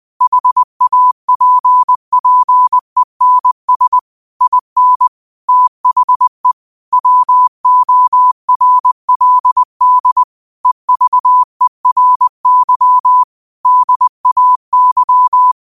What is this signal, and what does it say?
Morse code, heard as a keyed tone, about 20 wpm, 1000 Hz. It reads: HAPPENS IN THE WORLD EVERY DAY
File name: qotdfiles / 20wpm